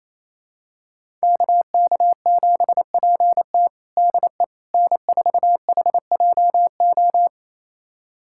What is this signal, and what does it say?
Morse code, heard as a keyed tone, about 28 wpm, 700 Hz. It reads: KK7PT DE N4HJO
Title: USA Callsign DE Callsign - 5 Characters - 28wpm